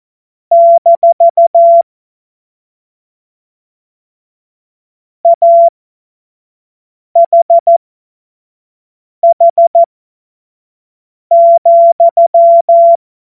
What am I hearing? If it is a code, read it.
Morse code, heard as a keyed tone, about 14 wpm, spaced as for 5 wpm, 680 Hz. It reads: - AHH,
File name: Morsekurs / Witze / Witz 17. Mai 2022 25 F70